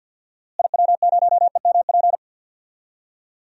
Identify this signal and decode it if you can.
Morse code, heard as a keyed tone, about 51 wpm, 700 Hz. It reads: IW0EMP
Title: DXCC-100 Call Signs - 6 Characters - Repeated 3x - 50wpm